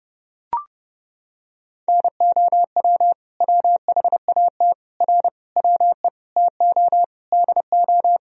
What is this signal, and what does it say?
Morse code, heard as a keyed tone, about 30 wpm, 700 Hz. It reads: NOW WHAT R WE TO DO
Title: Sentences from Top 100 Words and Abbreviations 30wpm